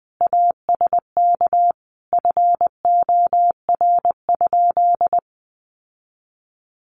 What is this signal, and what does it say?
Morse code, heard as a keyed tone, about 20 wpm, 700 Hz. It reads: ASK FOR?
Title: Sentences from Top 1400 Words 20wpm